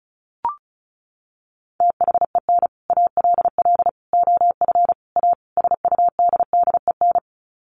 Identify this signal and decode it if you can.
Morse code, heard as a keyed tone, about 34 wpm, 700 Hz. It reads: THEN ALL OF A SUDDEN
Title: Sets of 5 Words Encore 35wpm